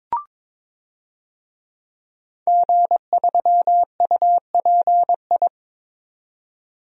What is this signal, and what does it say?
Morse code, heard as a keyed tone, about 22 wpm, 700 Hz. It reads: G3UPI
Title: DXCC-100 Call Signs - 5 Characters - 22wpm